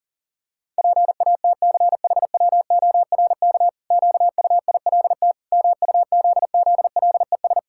Morse code, heard as a keyed tone, about 40 wpm, 700 Hz. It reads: PATCHWORK QUILT MUZZLES
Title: Pangram Challenge v2 - 40wpm